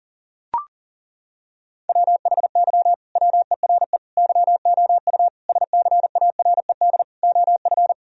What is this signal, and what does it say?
Morse code, heard as a keyed tone, about 40 wpm, 700 Hz. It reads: WHY WERE YOU SCARED OF
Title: Sentences from Top 1000 Words - Review Part 4 - 40wpm